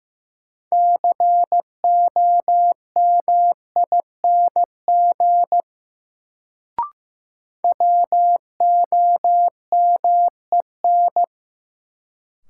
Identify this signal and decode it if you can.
Morse code, heard as a keyed tone, about 15 wpm, 700 Hz. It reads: COMING WOMEN
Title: Sentences from Top 1300 Words - New Word Drill - 15wpm